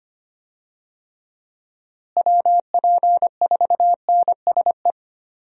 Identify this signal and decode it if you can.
Morse code, heard as a keyed tone, about 25 wpm, 700 Hz. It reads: WP4NSE
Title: USA Call Signs - 6 Characters - Repeated 2x - 25wpm